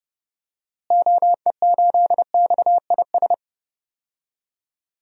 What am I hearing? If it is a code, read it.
Morse code, heard as a keyed tone, about 30 wpm, 700 Hz. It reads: OE8XIS
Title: DXCC-100 Call Signs - 6 Characters - Repeated 3x - 30wpm